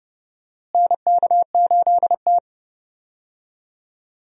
Morse code, heard as a keyed tone, about 31 wpm, 700 Hz. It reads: NK8T
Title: USA Call Signs - 4 Characters - Repeated 3x - 30wpm